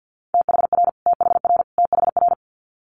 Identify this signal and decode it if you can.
Morse code, heard as a keyed tone, about 50 wpm, 690 Hz. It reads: THR THR THR
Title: Multi-letter Phonemes Repeated 3x 50wpm